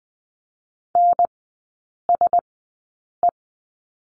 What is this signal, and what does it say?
Morse code, heard as a keyed tone, about 20 wpm, spaced as for 8 wpm, 700 Hz. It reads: NSE
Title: 041 Lesson 8e - Number 1 - Three Characters 20wpm